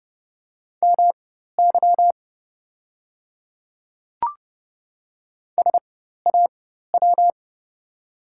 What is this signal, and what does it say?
Morse code, heard as a keyed tone, about 30 wpm, spaced as for 14 wpm, 700 Hz. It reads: MY SAW
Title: 4x Character Spacing - Top 300 Words - 30wpm